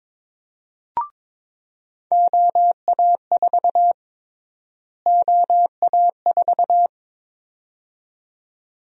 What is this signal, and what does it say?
Morse code, heard as a keyed tone, about 22 wpm, 700 Hz. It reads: OA4 OA4
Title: USA Call Sign Prefixes - Repeated 2x - 22wpm